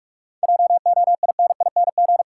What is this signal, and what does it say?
Morse code, heard as a keyed tone, about 46 wpm, 700 Hz. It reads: JOINING
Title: Sets of 5 Words - New Word Drill - 45wpm